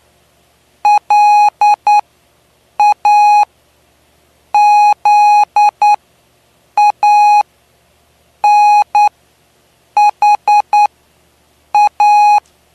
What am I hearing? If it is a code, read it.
Morse code, heard as a keyed tone, about 9 wpm, spaced as for 6 wpm, 830 Hz. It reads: LAZANHA
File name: cw / py2lcd / 09-PY2LCD-Li��o 9